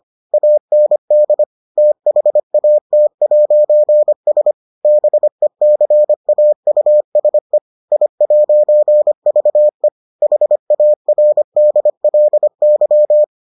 Morse code, heard as a keyed tone, about 25 wpm, 600 Hz. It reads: AND THAT'S BECAUSE I'VE HARDLY